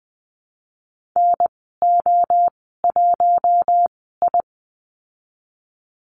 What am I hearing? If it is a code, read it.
Morse code, heard as a keyed tone, about 20 wpm, spaced as for 14 wpm, 700 Hz. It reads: NO1I
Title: 042 Lesson 8g - Number 1 - Call Signs 20wpm